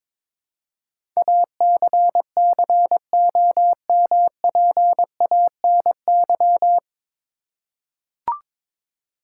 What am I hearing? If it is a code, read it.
Morse code, heard as a keyed tone, about 22 wpm, 700 Hz. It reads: ACCOMPANY
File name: 9 Letter Words - First 3 Letters Given - 22wpm